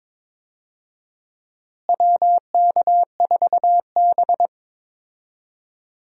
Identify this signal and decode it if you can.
Morse code, heard as a keyed tone, about 21 wpm, 700 Hz. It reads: WK4B
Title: USA Call Signs - 4 Characters - Repeated 3x - 22wpm